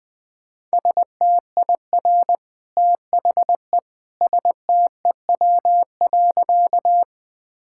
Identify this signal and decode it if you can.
Morse code, heard as a keyed tone, about 20 wpm, 700 Hz. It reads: STIR THE STEW.
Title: Easy Sentences with Hints - 20wpm